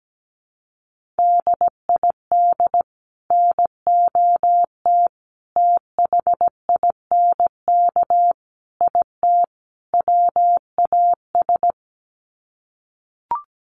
Morse code, heard as a keyed tone, about 17 wpm, 700 Hz. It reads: DID NOT THINK IT WAS
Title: Sets of 5 Words Encore 17wpm